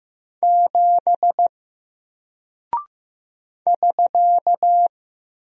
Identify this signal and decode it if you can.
Morse code, heard as a keyed tone, about 15 wpm, 700 Hz. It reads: 7 <SK>
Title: Single Character (Letter:Number:Punctuation:Prosign) Alt 15wpm